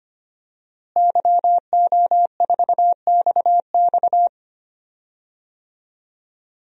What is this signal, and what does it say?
Morse code, heard as a keyed tone, about 25 wpm, 700 Hz. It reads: YO4XX
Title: DXCC-100 Call Signs - 5 Characters - Repeated 3x - 25wpm